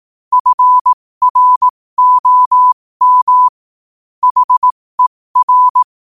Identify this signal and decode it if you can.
Morse code, heard as a keyed tone, about 18 wpm, spaced as for 15 wpm, 1000 Hz. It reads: FROM HER
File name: qotdfiles / 15wpm